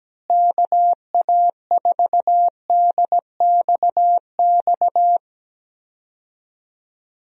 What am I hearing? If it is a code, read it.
Morse code, heard as a keyed tone, about 17 wpm, 700 Hz. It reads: KA4DXX
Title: USA Call Signs - 6 Characters - Repeated 3x - 17wpm